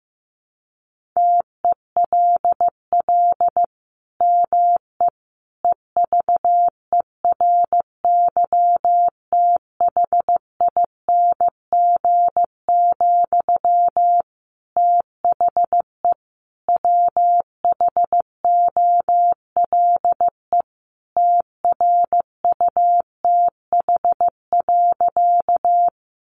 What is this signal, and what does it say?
Morse code, heard as a keyed tone, about 15 wpm, 700 Hz. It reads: TELL ME EVERYTHING, THE WHOLE TRUTH.
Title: Sentences from Top 800 Words 15wpm